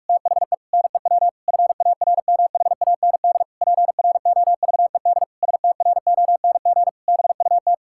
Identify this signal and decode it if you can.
Morse code, heard as a keyed tone, about 41 wpm, 700 Hz. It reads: THE NEW FARMHAND PROVED STRONG BUT
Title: Pangram Challenge v2 - 45wpm